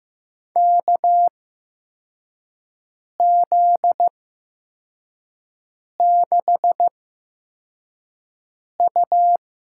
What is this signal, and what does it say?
Morse code, heard as a keyed tone, about 15 wpm, spaced as for 4 wpm, 700 Hz. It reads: KZ6U
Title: USA Call Signs - 4 Characters - 8x Character Spacing - 15wpm